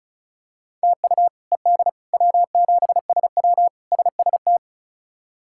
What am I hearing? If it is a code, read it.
Morse code, heard as a keyed tone, about 36 wpm, 700 Hz. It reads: TU ED W7SW SST
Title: K1USN SST QSO Practice - 35wpm